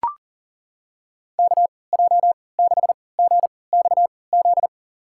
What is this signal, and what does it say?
Morse code, heard as a keyed tone, about 40 wpm, spaced as for 23 wpm, 700 Hz. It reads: KJ6GXZ
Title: USA Call Signs - 3x Character Spacing - 40wpm